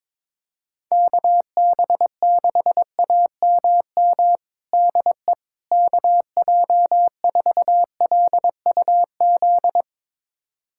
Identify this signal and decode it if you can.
Morse code, heard as a keyed tone, about 22 wpm, 700 Hz. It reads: KB6AMM DE KJ4LUZ
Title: USA Callsign DE Callsign - 6 Characters - 22wpm